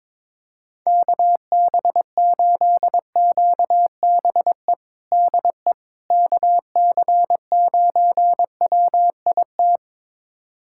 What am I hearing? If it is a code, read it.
Morse code, heard as a keyed tone, about 22 wpm, 700 Hz. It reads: KB8QBE DE KC9WIT
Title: USA Callsign DE Callsign 22wpm